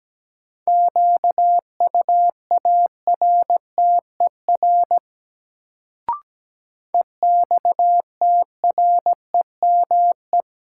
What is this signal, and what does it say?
Morse code, heard as a keyed tone, about 17 wpm, 700 Hz. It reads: QUARTER EXTREME
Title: Sentences from Top 1300 Words - New Word Drill - 17wpm